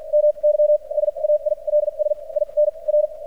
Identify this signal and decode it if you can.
Morse code, heard as a keyed tone, about 50 wpm, spaced as for 38 wpm, 600 Hz, with radio noise. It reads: MY SAILS IN R
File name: archived / CRSnet